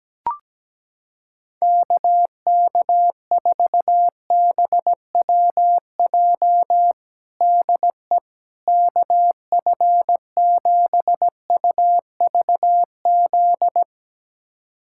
Morse code, heard as a keyed tone, about 17 wpm, 700 Hz. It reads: KK4BWJ DE KF7UVZ
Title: USA Callsign DE Callsign - 6 Characters - 17wpm